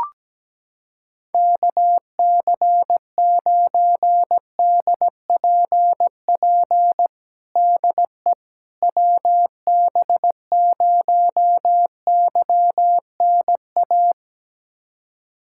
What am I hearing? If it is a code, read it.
Morse code, heard as a keyed tone, about 17 wpm, 700 Hz. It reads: KC9DPP DE WB0YNA
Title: USA Callsign DE Callsign - 6 Characters - 17wpm